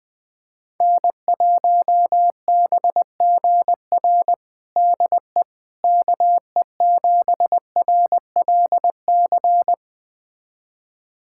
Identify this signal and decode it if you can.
Morse code, heard as a keyed tone, about 21 wpm, 700 Hz. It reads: N1BGR DE KE7RLC
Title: USA Callsign DE Callsign 20wpm